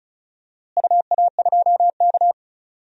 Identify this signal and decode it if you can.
Morse code, heard as a keyed tone, about 34 wpm, 700 Hz. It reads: UA2K
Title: DXCC-100 Call Signs - 4 Characters - Repeated 3x - 35wpm